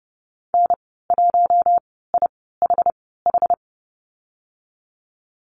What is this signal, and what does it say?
Morse code, heard as a keyed tone, about 30 wpm, spaced as for 17 wpm, 700 Hz. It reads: N1IHH
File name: Lesson 15g - Number 5 - Call Signs 30wpm